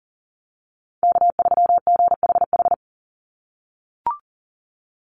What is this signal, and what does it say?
Morse code, heard as a keyed tone, about 40 wpm, 700 Hz. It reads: K3GHH
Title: Call Signs - 40wpm